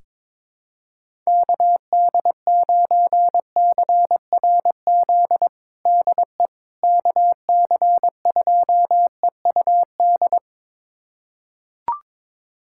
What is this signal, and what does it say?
Morse code, heard as a keyed tone, about 22 wpm, 700 Hz. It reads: KD9CRZ DE KC2EUD
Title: USA Callsign DE Callsign 22wpm